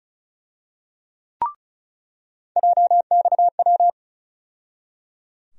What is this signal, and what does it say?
Morse code, heard as a keyed tone, about 36 wpm, 700 Hz. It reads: JXW
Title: USA Call Sign Postfixes - 3 Letters 35wpm